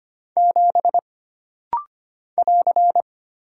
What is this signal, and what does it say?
Morse code, heard as a keyed tone, about 25 wpm, 700 Hz. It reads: 7 <AR>
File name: Single Character (Letter:Number:Punctuation:Prosign) - Rapid-Fire - 25wpm